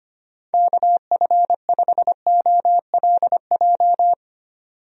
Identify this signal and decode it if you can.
Morse code, heard as a keyed tone, about 25 wpm, 700 Hz. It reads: KF5OLJ
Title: USA Call Signs - 6 Characters - 25wpm